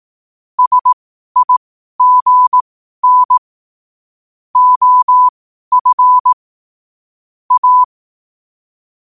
Morse code, heard as a keyed tone, about 18 wpm, spaced as for 12 wpm, 1000 Hz. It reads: SIGN OF A
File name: qotdfiles / 12wpm